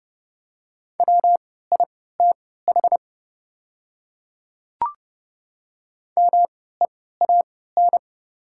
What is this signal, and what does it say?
Morse code, heard as a keyed tone, about 32 wpm, spaced as for 17 wpm, 700 Hz. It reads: WITH MEAN
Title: 3x Character Spacing - Top 200 Words - 30wpm